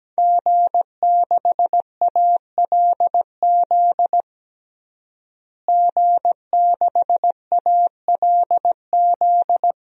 Morse code, heard as a keyed tone, about 17 wpm, 700 Hz. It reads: G6ALZ G6ALZ
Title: DXCC-100 Call Signs - 5 Characters - Repeated 2x - 17wpm